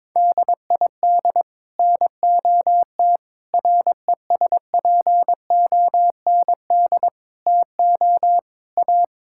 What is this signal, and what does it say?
Morse code, heard as a keyed tone, about 22 wpm, 700 Hz. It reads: DID NOT RESPOND TO A
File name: practice / 22wpm / Sets of 8 Words - 22wpm